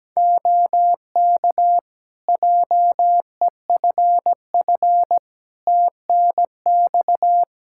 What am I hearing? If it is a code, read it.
Morse code, heard as a keyed tone, about 17 wpm, 700 Hz. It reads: OK JEFF TNX
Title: Practice QSO - Exchange 4 - Ending - 17wpm